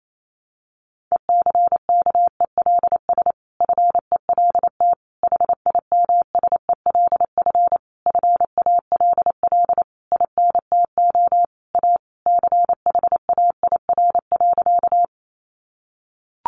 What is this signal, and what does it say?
Morse code, heard as a keyed tone, about 28 wpm, 700 Hz. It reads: ECKELS FELT HIMSELF FALL INTO A CHAIR.
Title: A Sound Of Thunder 28wpm